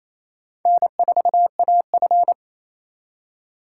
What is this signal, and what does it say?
Morse code, heard as a keyed tone, about 34 wpm, 700 Hz. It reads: N4AF
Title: Speed Racing - USA 1x2 Call Signs 28wpm